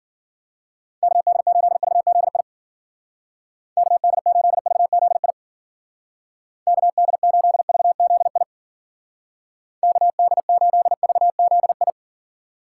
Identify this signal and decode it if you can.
Morse code, heard as a keyed tone, about 51 wpm, 700 Hz. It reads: KD8VZI KD8VZI KD8VZI KD8VZI